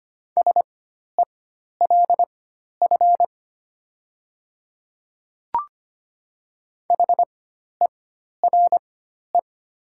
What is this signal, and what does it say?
Morse code, heard as a keyed tone, about 25 wpm, spaced as for 12 wpm, 690 Hz. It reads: SELF HERE
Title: 4x Character Spacing - Top 200 Words - 25wpm